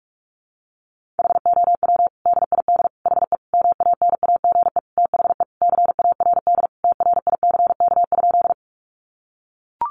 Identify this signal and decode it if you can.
Morse code, heard as a keyed tone, about 44 wpm, 700 Hz. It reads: HOW DID HE MANAGE THE CARD TRICK?
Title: Sentences from Top 1500 Words 45wpm